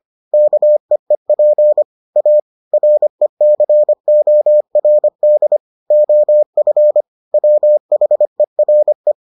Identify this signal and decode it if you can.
Morse code, heard as a keyed tone, about 25 wpm, 600 Hz. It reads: KEEP A RECORD OF WHERE